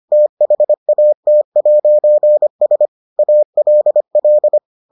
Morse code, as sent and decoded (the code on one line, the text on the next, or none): - .... .- - .----. ... / .- .-.. .-..
THAT'S ALL